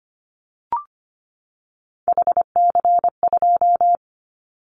Morse code, heard as a keyed tone, about 25 wpm, 700 Hz. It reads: HC2
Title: Callsign Prefixes for Top 100 DXCC - 25wpm